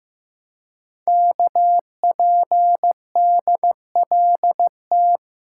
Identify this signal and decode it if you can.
Morse code, heard as a keyed tone, about 15 wpm, 700 Hz. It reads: KPDLT
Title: Code Groups - 5 Letters - 15wpm